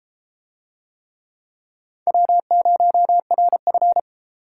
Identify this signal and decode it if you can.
Morse code, heard as a keyed tone, about 29 wpm, 700 Hz. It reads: W0RF